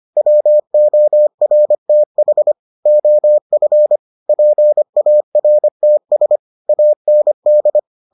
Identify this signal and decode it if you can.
Morse code, heard as a keyed tone, about 25 wpm, 600 Hz. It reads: WORTH OF PARTS AND